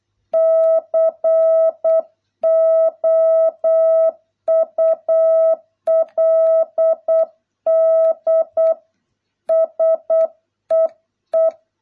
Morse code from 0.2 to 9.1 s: COULD